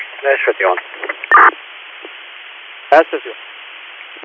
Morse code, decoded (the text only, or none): none